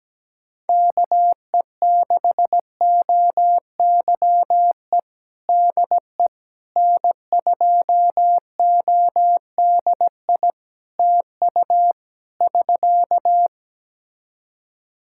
Morse code, heard as a keyed tone, about 17 wpm, 700 Hz. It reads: KE6OYE DE N2ODI TU <SK>
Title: K1USN SST QSO Practice 17wpm